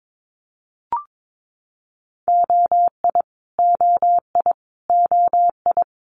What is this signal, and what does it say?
Morse code, heard as a keyed tone, about 22 wpm, 700 Hz. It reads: OI OI OI
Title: Multi-letter Phonemes Repeated 3x 22wpm